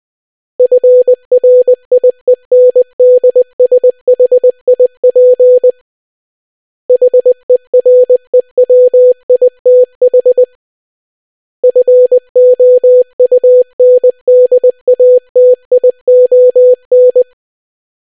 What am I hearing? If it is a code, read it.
Morse code, heard as a keyed tone, about 20 wpm, 500 Hz. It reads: FRIENDSHIP HEREWITH FOUNDATION